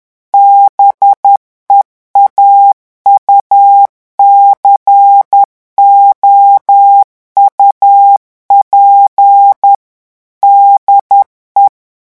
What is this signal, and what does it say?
Morse code, heard as a keyed tone, about 10 wpm, 800 Hz. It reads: BEAUCOUP DE
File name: revision9 citations humoristiques vitesse 10 mots minute